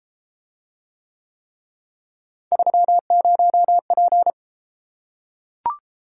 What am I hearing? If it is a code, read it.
Morse code, heard as a keyed tone, about 33 wpm, 700 Hz. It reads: 30P